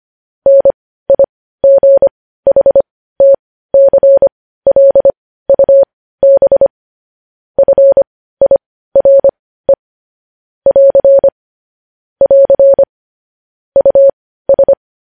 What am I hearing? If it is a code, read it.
Morse code, heard as a keyed tone, about 25 wpm, spaced as for 15 wpm, 550 Hz. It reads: NIGHTCLUB FIRE <AR> <AR> US
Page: news-headlines-in-morse-code-at-15-wpm